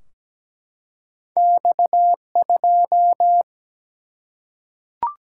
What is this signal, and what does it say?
Morse code, heard as a keyed tone, about 17 wpm, 700 Hz. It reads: X2
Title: Single-Letter Single-Number 17wpm